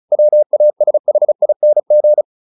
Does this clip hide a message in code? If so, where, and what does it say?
Yes: Morse code from 0.1 to 2.2 s, heard as a keyed tone, about 34 wpm, 600 Hz. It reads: WASHING